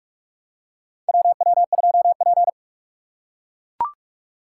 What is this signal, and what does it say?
Morse code, heard as a keyed tone, about 42 wpm, 700 Hz. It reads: WW2P
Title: K1USN SST QSO Practice - 45wpm